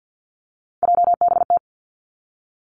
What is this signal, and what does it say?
Morse code, heard as a keyed tone, about 50 wpm, 700 Hz. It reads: JBT
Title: Three Challenging Letter-Number Combinations 50wpm